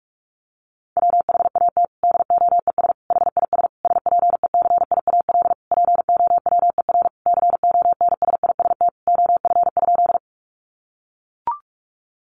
Morse code, heard as a keyed tone, about 44 wpm, 700 Hz. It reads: WHAT DOES HIS SPECIAL POWER CONSIST OF?
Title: Sentences from Top 1400 Words 45wpm